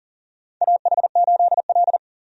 Morse code, heard as a keyed tone, about 40 wpm, 700 Hz. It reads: AH8L
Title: USA Call Signs - 4 Characters - 40wpm